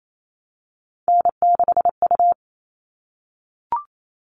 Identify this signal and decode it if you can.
Morse code, heard as a keyed tone, about 28 wpm, 700 Hz. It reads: N6U
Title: All Cut Numbers - 3-Digit 28wpm